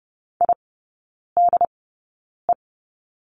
Morse code, heard as a keyed tone, about 30 wpm, spaced as for 9 wpm, 700 Hz. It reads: IDE
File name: Lesson 18e - Letter m - Three Characters 30wpm